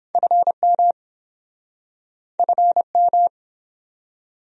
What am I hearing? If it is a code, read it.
Morse code, heard as a keyed tone, about 29 wpm, 700 Hz. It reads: FM FM